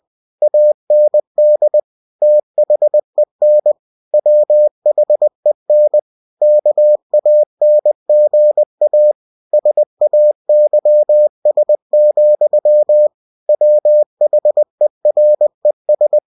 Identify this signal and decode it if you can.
Morse code, heard as a keyed tone, about 20 wpm, 600 Hz. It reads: AND THEN WHEN KANGA SAYS, WHERES
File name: cw-books / winnie-the-pooh / winnie-20wpm-page-0061